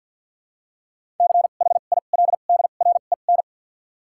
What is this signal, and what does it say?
Morse code, heard as a keyed tone, about 52 wpm, spaced as for 36 wpm, 700 Hz. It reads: CHILDREN
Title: 2x Character Spacing - Top 300 Words - 50wpm